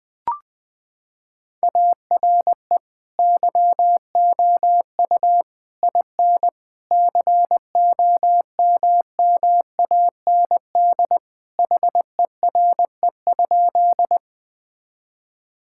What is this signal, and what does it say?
Morse code, heard as a keyed tone, about 21 wpm, 700 Hz. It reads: ARE YOU IN COMMAND HERE?
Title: Sentences from Top 1600 Words 20wpm